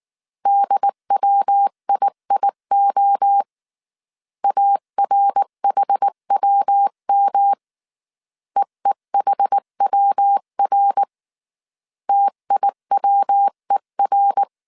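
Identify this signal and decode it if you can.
Morse code, heard as a keyed tone, about 19 wpm, 800 Hz. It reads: DWIIO ARHWM EEHWR TIWER
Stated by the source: Morse code dictation at 18 wpm from the IW3SGT QRS Morse code training course, five-character groups of 300 characters.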